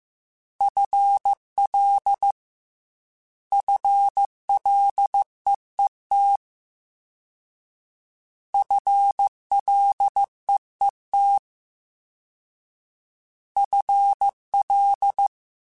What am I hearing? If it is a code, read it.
Morse code, heard as a keyed tone, about 15 wpm, 800 Hz. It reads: FL FLEET FLEET FL
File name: Combinations / Combinations-mp3s / Combinations@13